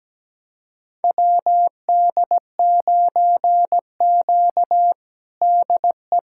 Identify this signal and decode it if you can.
Morse code, heard as a keyed tone, about 17 wpm, 700 Hz. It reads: WD9Q DE
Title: USA Callsign DE Callsign - 4 Characters - 17wpm